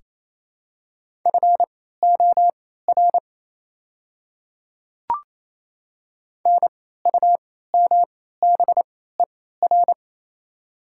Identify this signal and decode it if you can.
Morse code, heard as a keyed tone, about 27 wpm, spaced as for 16 wpm, 700 Hz. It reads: FOR NUMBER
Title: 3x Character Spacing - Top 100 Words - 28wpm